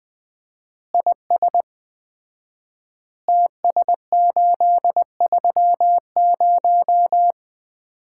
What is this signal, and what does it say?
Morse code, heard as a keyed tone, about 20 wpm, 700 Hz. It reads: IS TS830
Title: Common QSO Phrases - 4x Word-Spacing - 20wpm